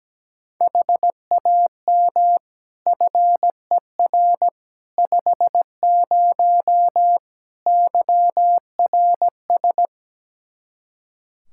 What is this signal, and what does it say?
Morse code, heard as a keyed tone, about 17 wpm, 700 Hz. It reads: HAM FER 50 YRS
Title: Practice QSO - Exchange 3 - More Info - 17wpm